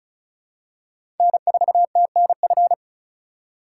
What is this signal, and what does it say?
Morse code, heard as a keyed tone, about 34 wpm, 700 Hz. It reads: N4TNF
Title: USA Call Signs - 5 Characters - 3x Character Spacing - 35wpm